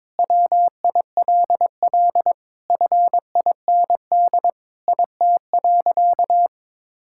Morse code, heard as a keyed tone, about 22 wpm, 700 Hz. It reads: WILL FIND IT.
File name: practice / 22wpm / Sentences with 3-Word Phrases - 22wpm